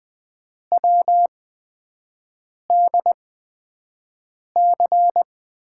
Morse code, heard as a keyed tone, about 20 wpm, spaced as for 5 wpm, 700 Hz. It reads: WDC
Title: USA Call Signs - 8x Character Spacing - 20wpm